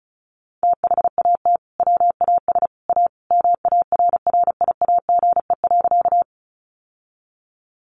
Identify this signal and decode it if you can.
Morse code, heard as a keyed tone, about 34 wpm, 700 Hz. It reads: THAT WAS A MARRIAGE.
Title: Sentences from Top 800 Words 35wpm